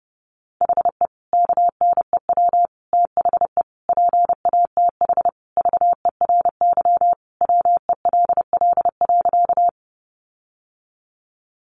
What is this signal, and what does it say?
Morse code, heard as a keyed tone, about 28 wpm, 700 Hz. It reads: HE KNEW THE PATH VERY WELL.
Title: Sentences from Top 1400 Words 30wpm